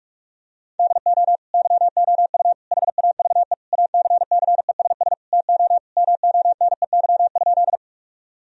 Morse code, heard as a keyed tone, about 41 wpm, 700 Hz. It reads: DO YOU HAVE ACCESS TO MONEY?
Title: Sentences from Top 1000 Words - Review Part 4 - 45wpm